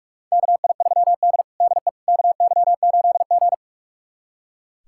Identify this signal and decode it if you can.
Morse code, heard as a keyed tone, about 41 wpm, 700 Hz. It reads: KI3D DE KY8G
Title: USA Callsign DE Callsign - 4 Characters - 45wpm